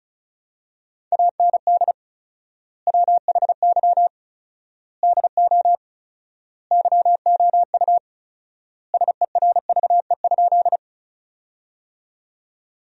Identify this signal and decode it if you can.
Morse code, heard as a keyed tone, about 36 wpm, 700 Hz. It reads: AND WHY DO YOU SERVE?
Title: 4x Word-Spacing - Sentences from Top 400 Words - 35wpm